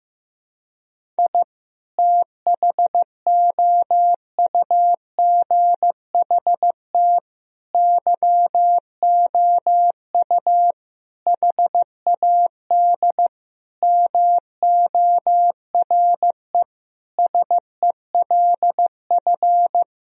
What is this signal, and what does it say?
Morse code, heard as a keyed tone, about 15 wpm, 700 Hz. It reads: I THOUGHT YOU HAD MORE SELF
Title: Sentences from Top 1000 Words - Review Part 3 - 15wpm